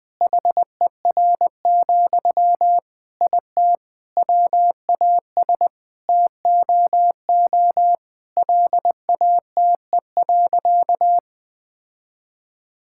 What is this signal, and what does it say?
Morse code, heard as a keyed tone, about 21 wpm, 700 Hz. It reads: HER, IT WAS TOO LATE.
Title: Sentences from Top 1000 Words - Review Part 6 - 20wpm